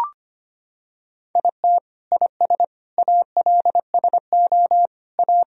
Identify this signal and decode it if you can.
Morse code, heard as a keyed tone, about 25 wpm, 700 Hz. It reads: IT IS ALSO A
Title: Sentences from Top 1000 Words - Review Part 1 - 25wpm